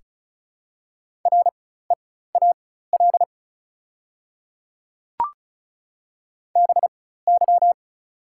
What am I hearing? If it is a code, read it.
Morse code, heard as a keyed tone, about 36 wpm, spaced as for 16 wpm, 690 Hz. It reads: REAL BY
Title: 4x Character Spacing - Top 300 Words - 35wpm